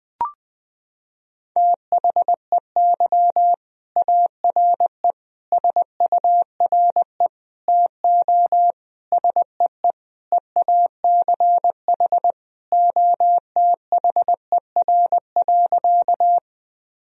Morse code, heard as a keyed tone, about 20 wpm, 700 Hz. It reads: THEY ARE SURE TO SEE EACH OTHER.
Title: Sentences from Top 1000 Words - Review Part 1 - 20wpm